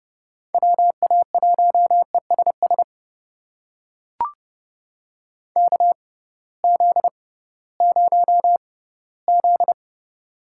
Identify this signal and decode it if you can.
Morse code, heard as a keyed tone, about 31 wpm, spaced as for 18 wpm, 700 Hz. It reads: WA1ESS KZ0Z
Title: USA Call Signs - 6x Character Spacing - 30wpm